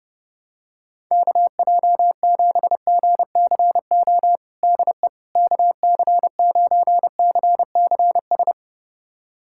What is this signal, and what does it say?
Morse code, heard as a keyed tone, about 32 wpm, 700 Hz. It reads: KJ7GCO DE KC9CCS
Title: USA Callsign DE Callsign 30wpm